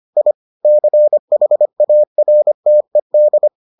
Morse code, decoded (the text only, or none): I CHARTED